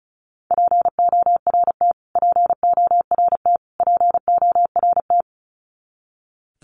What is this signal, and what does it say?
Morse code, heard as a keyed tone, about 34 wpm, 700 Hz. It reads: PORT PORT PORT
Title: Top 200 Words Repeated 35wpm